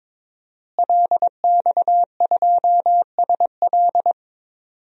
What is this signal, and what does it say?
Morse code, heard as a keyed tone, about 22 wpm, 700 Hz. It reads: LX2SL
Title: DXCC-100 Call Signs - 5 Characters - Repeated 3x - 22wpm